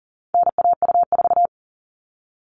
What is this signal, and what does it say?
Morse code, heard as a keyed tone, about 40 wpm, 700 Hz. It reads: NAU4
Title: All Cut Numbers - 4-Digit 40wpm